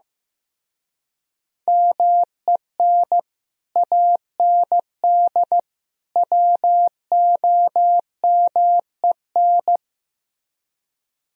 Morse code, heard as a keyed tone, about 16 wpm, 700 Hz. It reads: MEN AND WOMEN
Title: Speed-Racing - Sets of 3 Words 15wpm